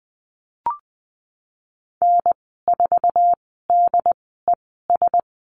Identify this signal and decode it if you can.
Morse code, heard as a keyed tone, about 20 wpm, spaced as for 14 wpm, 700 Hz. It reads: N4DES
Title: 082 Lesson 14g - Number 2 - Call Signs 20wpm